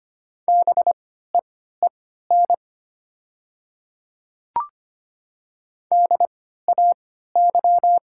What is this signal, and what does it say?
Morse code, heard as a keyed tone, about 25 wpm, spaced as for 14 wpm, 700 Hz. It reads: BEEN DAY
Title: 3x Character Spacing - Top 100 Words - 25wpm